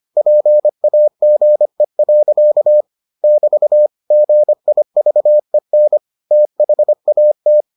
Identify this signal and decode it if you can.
Morse code, heard as a keyed tone, about 25 wpm, 600 Hz. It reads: PAGE. <BT> GIVEN THAT